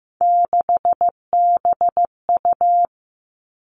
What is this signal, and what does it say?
Morse code, heard as a keyed tone, about 15 wpm, 700 Hz. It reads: 6BU
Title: Three Challenging Letter-Number Combinations 15wpm